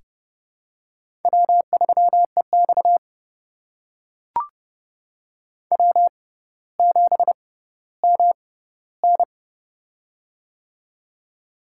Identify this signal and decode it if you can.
Morse code, heard as a keyed tone, about 30 wpm, spaced as for 15 wpm, 700 Hz. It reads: W3EX W7MN
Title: USA Call Signs - 4 Characters - 6x Character Spacing - 30wpm